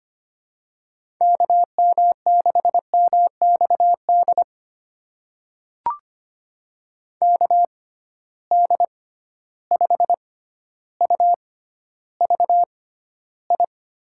Morse code, heard as a keyed tone, about 25 wpm, spaced as for 13 wpm, 700 Hz. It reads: KM6MXD KD5UVI
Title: USA Call Signs - 6 Characters - 6x Character Spacing - 25wpm